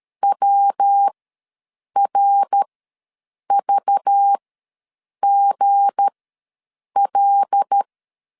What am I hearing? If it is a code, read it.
Morse code, heard as a keyed tone, about 13 wpm, spaced as for 7 wpm, 800 Hz. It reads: WRVGL
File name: LEZIONI / 7 GF9Z / 7 G ws 12wpm 3140